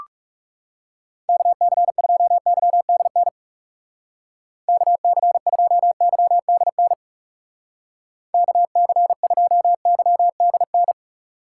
Morse code, heard as a keyed tone, about 39 wpm, 700 Hz. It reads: KC2YDN KC2YDN KC2YDN